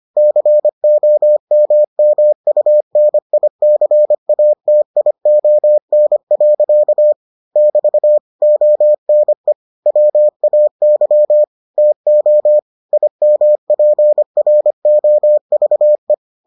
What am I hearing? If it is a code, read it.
Morse code, heard as a keyed tone, about 25 wpm, 600 Hz. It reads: COMMUNICATION. <BT> ONE WAY TO IMPROVE